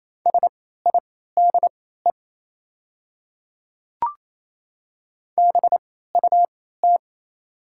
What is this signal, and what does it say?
Morse code, heard as a keyed tone, about 27 wpm, spaced as for 16 wpm, 690 Hz. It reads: SIDE BUT
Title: 3x Character Spacing - Top 100 Words - 28wpm